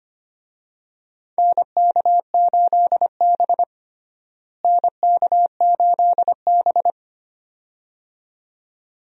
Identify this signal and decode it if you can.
Morse code, heard as a keyed tone, about 25 wpm, 700 Hz. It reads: NK8B NK8B
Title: USA Call Signs - 4 Characters - Repeated 2x - 25wpm